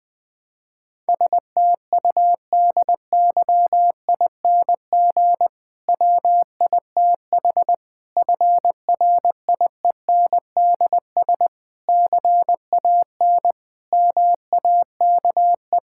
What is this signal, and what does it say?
Morse code, heard as a keyed tone, about 20 wpm, 700 Hz. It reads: STUDYING WITH FRIENDS CAN MAKE
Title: Difficult Sentences with Hints - 20wpm